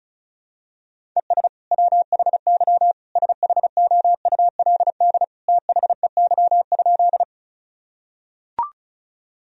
Morse code, heard as a keyed tone, about 36 wpm, 700 Hz. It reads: ES WHY SHOULD THEY?
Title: Sentences from Top 300 Words and Abbreviations 35wpm